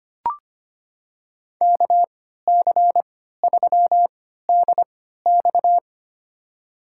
Morse code, heard as a keyed tone, about 25 wpm, spaced as for 14 wpm, 700 Hz. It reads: KC3DX
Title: USA Call Signs - 5 Characters - 3x Character Spacing - 25wpm